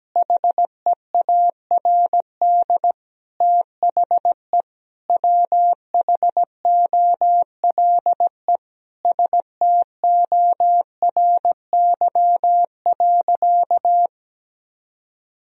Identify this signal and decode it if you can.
Morse code, heard as a keyed tone, about 17 wpm, 700 Hz. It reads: HEARD THE WHOLE STORY.
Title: Sentences from Top 1000 Words - Review Part 3 - 17wpm